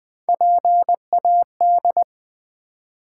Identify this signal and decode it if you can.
Morse code, heard as a keyed tone, about 20 wpm, 700 Hz. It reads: PAD
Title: Build A Word - 1 Letter At A Time - 3-Letter Words - 20wpm